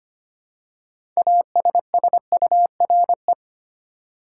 Morse code, heard as a keyed tone, about 25 wpm, 700 Hz. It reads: ASSURE
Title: Six Letter Words - 25wpm